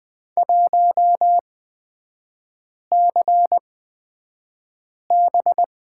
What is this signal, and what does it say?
Morse code, heard as a keyed tone, about 20 wpm, spaced as for 5 wpm, 700 Hz. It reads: 1CB
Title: 255 Lesson 41e - Period - Three Characters 20wpm